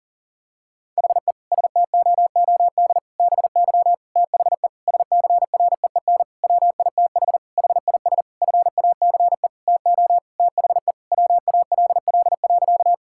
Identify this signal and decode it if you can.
Morse code, heard as a keyed tone, about 40 wpm, 700 Hz. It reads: HE STOOD BY THE SCREEN WITH HIS FACE TO THE WALL.
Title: Sentences from Top 1000 Words - Review Part 6 - 40wpm